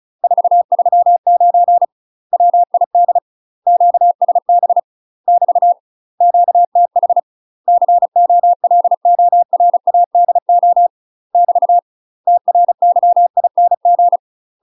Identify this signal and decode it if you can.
Morse code, heard as a keyed tone, about 36 wpm, 700 Hz. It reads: 439 WID QSB <BT> QTH COLORADO <BT> TRYING